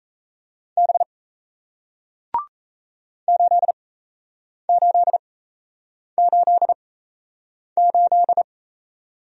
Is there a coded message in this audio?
Yes: Morse code from 0.8 to 8.4 s, heard as a keyed tone, about 38 wpm, spaced as for 8 wpm, 700 Hz. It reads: B 8888